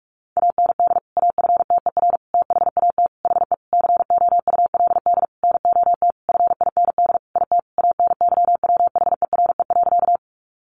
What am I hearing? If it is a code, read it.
Morse code, heard as a keyed tone, about 44 wpm, 700 Hz. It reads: AND AFTER THAT HE COULD NOT FIND IT ANYWHERE.
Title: Sentences from Top 1400 Words 45wpm